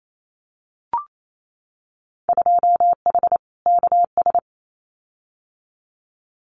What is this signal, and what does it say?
Morse code, heard as a keyed tone, about 28 wpm, 700 Hz. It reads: 2H KS
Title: Winter Field Day Exchanges Encore 28wpm